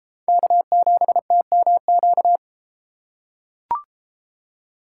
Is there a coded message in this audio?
Yes: Morse code from 0.3 to 2.4 s, heard as a keyed tone, about 33 wpm, 700 Hz. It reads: K7TMQ